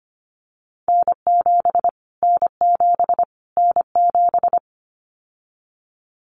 Morse code, heard as a keyed tone, about 25 wpm, 700 Hz. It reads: N7 N7 N7
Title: Callsign Prefixes for Top 25 DXCC - Repeated 3x - 25wpm